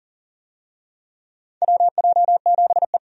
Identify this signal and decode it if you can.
Morse code, heard as a keyed tone, about 40 wpm, 700 Hz. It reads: WJ7E
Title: USA Call Signs - 4 Characters - 40wpm